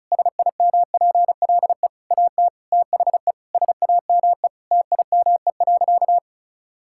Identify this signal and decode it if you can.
Morse code, heard as a keyed tone, about 36 wpm, 700 Hz. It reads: SIMPLE AT THE SAME TIME.
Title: Sentences from Top 1000 Words - Review Part 6 - 35wpm